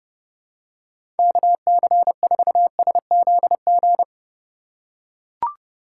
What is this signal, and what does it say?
Morse code, heard as a keyed tone, about 32 wpm, 700 Hz. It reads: KC4SZG
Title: USA Call Signs - 6 Characters - 4x Character Spacing - 30wpm